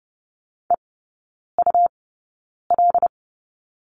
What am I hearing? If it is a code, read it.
Morse code, heard as a keyed tone, about 30 wpm, spaced as for 9 wpm, 700 Hz. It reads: EUL
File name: Lesson 18e - Letter m - Three Characters 30wpm